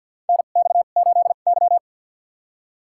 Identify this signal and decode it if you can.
Morse code, heard as a keyed tone, about 47 wpm, spaced as for 35 wpm, 700 Hz. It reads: NX8Y
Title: USA Call Signs - 4 Characters - 2x Character Spacing - 50wpm